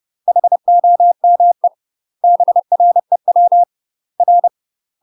Morse code, heard as a keyed tone, about 30 wpm, 700 Hz. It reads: HOME BREW R
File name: UK MASTER QSOs TRAINING / 30WPM / UK MASTER QSOs 0051